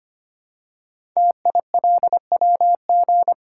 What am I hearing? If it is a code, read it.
Morse code, heard as a keyed tone, about 25 wpm, 700 Hz. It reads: TILWG
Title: Code Groups - 5 Letters - 25wpm